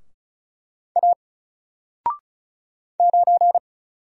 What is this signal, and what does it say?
Morse code, heard as a keyed tone, about 34 wpm, 700 Hz. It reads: A 9
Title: Single Letter-Number Alt 35wpm